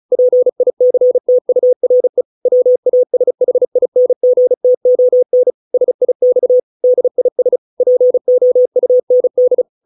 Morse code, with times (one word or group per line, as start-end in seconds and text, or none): 0.1-2.2 PICTURE
2.5-5.5 WASHINGTON
5.7-6.6 SIX
6.8-7.6 DIS
7.8-9.6 POUND